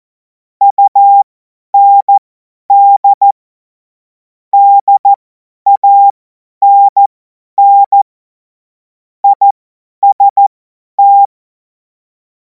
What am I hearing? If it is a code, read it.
Morse code, heard as a keyed tone, about 14 wpm, spaced as for 10 wpm, 800 Hz. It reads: UND DANN IST
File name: Morsekurs / Witze / Witz 10. Februar 2026 50 F70